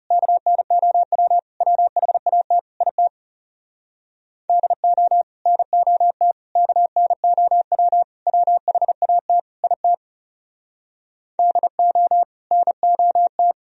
Morse code, heard as a keyed tone, about 39 wpm, 700 Hz. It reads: KNOW WHAT IT DO NOT KNOW WHAT IT DO NOT